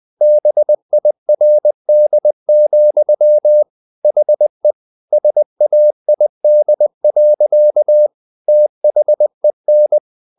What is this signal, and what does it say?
Morse code, heard as a keyed tone, about 20 wpm, 600 Hz. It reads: BIRD, HE SAID. THEN
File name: the old man and the sea / the old man and the sea part 1 20wpm